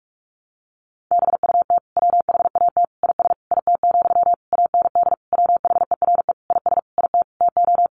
Morse code, heard as a keyed tone, about 44 wpm, 700 Hz. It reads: BUT WHAT IS IT, AND WHERE IS IT TO
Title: Sentences from Top 300 Words 45wpm